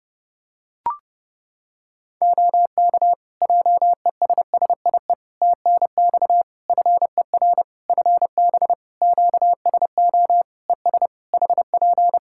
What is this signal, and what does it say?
Morse code, heard as a keyed tone, about 30 wpm, 700 Hz. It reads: OK JESSIE TNX FER FB QSO ES HP
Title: Practice QSO - Exchange 4 - Ending - 30wpm